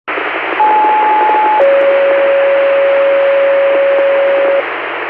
Two-tone page, heard as a one-second tone, then a three-second tone, about 860 then 550 Hz, with noise. It is heard from 0.6 to 4.6 s.